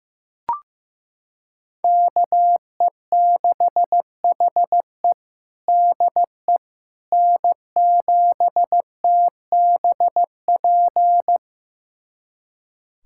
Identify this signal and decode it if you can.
Morse code, heard as a keyed tone, about 16 wpm, 700 Hz. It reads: KE6HE DE N7TBP
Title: USA Callsign DE Callsign - 5 Characters - 15wpm